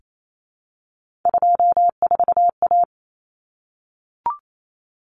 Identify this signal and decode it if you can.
Morse code, heard as a keyed tone, about 28 wpm, 700 Hz. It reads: 24A
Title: Common Cut Numbers - 3-Digit 28wpm